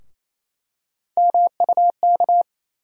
Morse code, heard as a keyed tone, about 28 wpm, 700 Hz. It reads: MUK
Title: USA Call Sign Postfixes - 3 Letters 28wpm